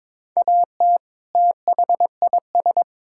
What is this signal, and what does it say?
Morse code, heard as a keyed tone, about 21 wpm, 700 Hz. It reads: AT THIS
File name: Build A Phrase - 2-Word Phrases - 22wpm